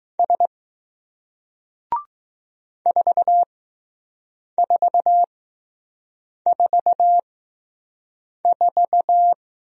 Morse code, heard as a keyed tone, about 20 wpm, spaced as for 6 wpm, 700 Hz. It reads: S 4444